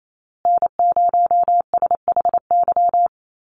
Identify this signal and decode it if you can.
Morse code, heard as a keyed tone, about 28 wpm, 700 Hz. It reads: N0SHY
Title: USA Call Signs Repeated 2x 28wpm